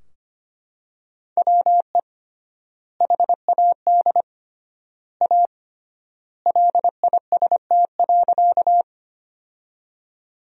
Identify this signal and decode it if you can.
Morse code, heard as a keyed tone, about 25 wpm, 700 Hz. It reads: WE HAD A LIST.
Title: 3x Word-Spacing - Sentences from Top 400 Words - 25wpm